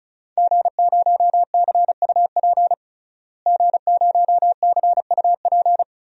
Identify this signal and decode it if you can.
Morse code, heard as a keyed tone, about 36 wpm, 700 Hz. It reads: G0CUP G0CUP
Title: DXCC-100 Call Signs - 5 Characters - Repeated 3x - 35wpm